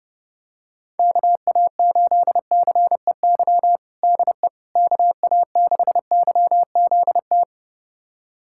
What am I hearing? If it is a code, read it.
Morse code, heard as a keyed tone, about 30 wpm, 700 Hz. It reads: KA8CEY DE KA6YZT
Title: USA Callsign DE Callsign - 6 Characters - 30wpm